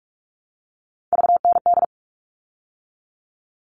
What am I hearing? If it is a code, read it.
Morse code, heard as a keyed tone, about 44 wpm, 700 Hz. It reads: VND
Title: All Cut Numbers - 3-Digit 45wpm